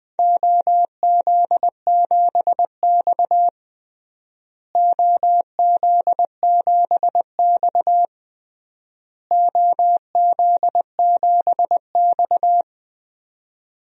Morse code, heard as a keyed tone, about 21 wpm, 700 Hz. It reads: OZ7X OZ7X OZ7X
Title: DXCC-100 Call Signs - 4 Characters - Repeated 3x - 20wpm